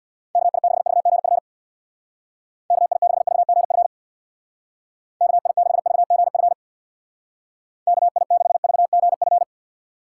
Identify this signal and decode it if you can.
Morse code, heard as a keyed tone, about 64 wpm, 700 Hz. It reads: KI6VGF KI6VGF KI6VGF KI6VGF